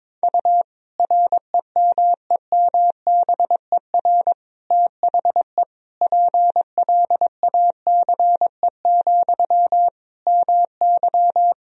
Morse code, heard as a keyed tone, about 22 wpm, 700 Hz. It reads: U REMEMBER THE PLACE, MY
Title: Sentences from Top 400 Words and Abbreviations 22wpm